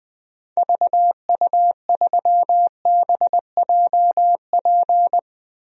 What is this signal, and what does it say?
Morse code, heard as a keyed tone, about 21 wpm, 700 Hz. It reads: VU3BJP
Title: DXCC-100 Call Signs - 6 Characters - 20wpm